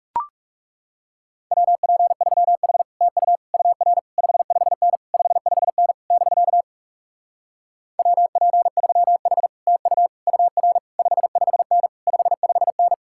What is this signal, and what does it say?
Morse code, heard as a keyed tone, about 38 wpm, 700 Hz. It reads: WP3H TU UR 55N 55N <BK> WP3H TU UR 55N 55N
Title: Speed-Racing Common QSO Phrases - 30wpm